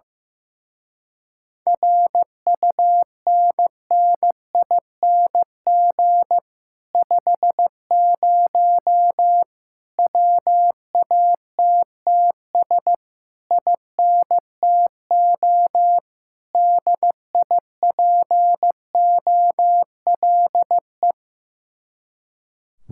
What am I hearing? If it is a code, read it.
Morse code, heard as a keyed tone, about 15 wpm, 700 Hz. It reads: RUNNING 50 WATTS INTO DIPOLE
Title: Speed-Racing Common QSO Phrases Encore - 15wpm